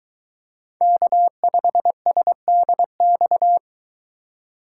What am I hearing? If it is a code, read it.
Morse code, heard as a keyed tone, about 21 wpm, 700 Hz. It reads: K5SDX